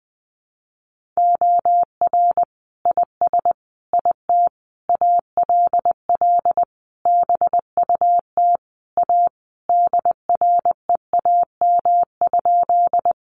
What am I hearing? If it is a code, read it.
Morse code, heard as a keyed tone, about 20 wpm, 700 Hz. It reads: OR IS IT ALL BUT A DREAM?
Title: Sentences from Top 1500 Words 20wpm